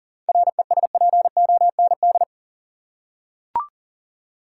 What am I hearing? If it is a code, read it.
Morse code, heard as a keyed tone, about 40 wpm, 700 Hz. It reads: RESPOND
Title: Seven Letter Words - 40wpm